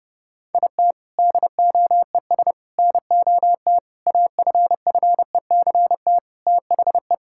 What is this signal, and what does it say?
Morse code, heard as a keyed tone, about 32 wpm, 700 Hz. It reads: IT DOES NOT AFFECT THE
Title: Sentences from Top 1000 Words - Review Part 6 - 30wpm